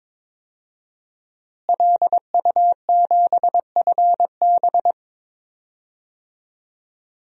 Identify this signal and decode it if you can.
Morse code, heard as a keyed tone, about 22 wpm, 700 Hz. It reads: LU7FB
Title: DXCC-100 Call Signs - 5 Characters - Repeated 3x - 22wpm